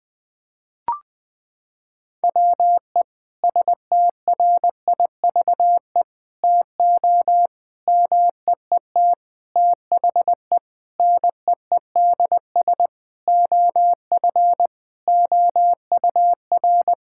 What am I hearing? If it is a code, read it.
Morse code, heard as a keyed tone, about 20 wpm, 700 Hz. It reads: WE STRIVE TO MEET THE NEEDS OF OUR
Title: Sentences with 5-Word Phrases - 20wpm